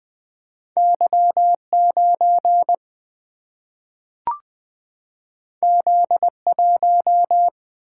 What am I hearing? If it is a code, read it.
Morse code, heard as a keyed tone, about 20 wpm, 700 Hz. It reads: Y9 Z1
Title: Single-Letter Single-Number 20wpm